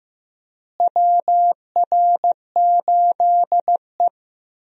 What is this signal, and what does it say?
Morse code, heard as a keyed tone, about 15 wpm, 700 Hz. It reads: WR8E
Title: USA Call Signs - 4 Characters - Repeated 3x - 15wpm